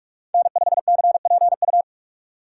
Morse code, heard as a keyed tone, about 44 wpm, 690 Hz. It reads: N5CPU
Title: USA Call Signs - 5 Characters - 6x Character Spacing - 45wpm